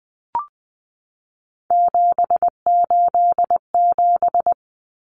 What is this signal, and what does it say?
Morse code, heard as a keyed tone, about 20 wpm, 700 Hz. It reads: 787
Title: Common Cut Numbers - 3-Digit 20wpm